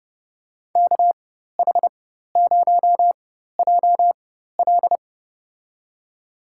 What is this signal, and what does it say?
Morse code, heard as a keyed tone, about 30 wpm, spaced as for 14 wpm, 700 Hz. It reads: KH0JL
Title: USA Call Signs - 4x Character Spacing - 30wpm